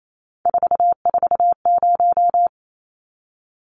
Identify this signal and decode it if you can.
Morse code, heard as a keyed tone, about 28 wpm, 700 Hz. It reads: 440
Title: Numbers - 3-Digit 28wpm